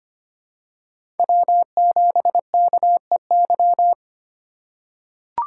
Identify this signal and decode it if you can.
Morse code, heard as a keyed tone, about 25 wpm, 700 Hz. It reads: W7KEY
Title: USA Call Signs - 2x Character Spacing - 25wpm